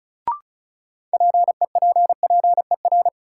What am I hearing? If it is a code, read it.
Morse code, heard as a keyed tone, about 34 wpm, 700 Hz. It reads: PEPPER
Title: Sets of 5 Words - New Word Drill - 35wpm